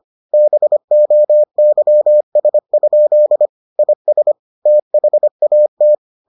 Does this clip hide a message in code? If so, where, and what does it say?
Yes: Morse code from 0.3 to 6.0 s, heard as a keyed tone, about 25 wpm, 600 Hz. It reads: BOYS? IS THAT